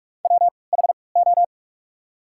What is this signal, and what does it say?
Morse code, heard as a keyed tone, about 45 wpm, spaced as for 25 wpm, 700 Hz. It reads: WHO
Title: 3x Character Spacing - Top 500 Words - 45wpm